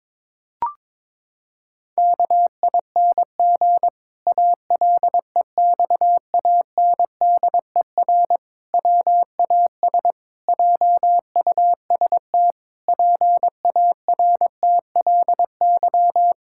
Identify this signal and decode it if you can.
Morse code, heard as a keyed tone, about 22 wpm, 700 Hz. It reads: KING ALEXANDER WAS JUST PARTLY
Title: Pangram Challenge v2 - 22wpm